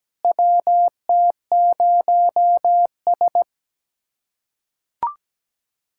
Morse code, heard as a keyed tone, about 17 wpm, 700 Hz. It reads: WT0S
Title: USA Callsign DE Callsign - 4 Characters - 17wpm